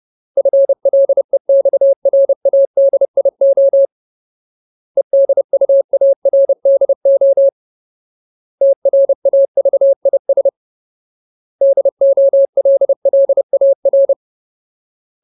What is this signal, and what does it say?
Morse code, heard as a keyed tone, about 30 wpm, 550 Hz. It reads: FLEXRADIO EDUARDO TRAVIS DOLLAR
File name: Ultimate Word List / 30WPM Ultimate